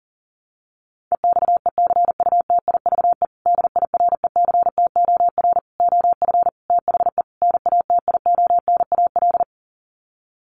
Morse code, heard as a keyed tone, about 40 wpm, 700 Hz. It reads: EXECUTIVE DIRECTOR OF THE NATIONAL
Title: Sets of 5 Words Encore2 40wpm